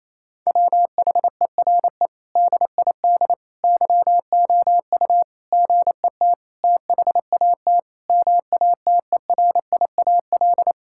Morse code, heard as a keyed tone, about 28 wpm, 700 Hz. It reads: WHERE DID YOU GET THAT MATERIAL
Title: Sentences from Top 1000 Words - Review Part 5 - 28wpm